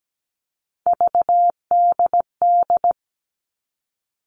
Morse code, heard as a keyed tone, about 17 wpm, 700 Hz. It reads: VDD
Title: All Cut Numbers - 3-Digit 17wpm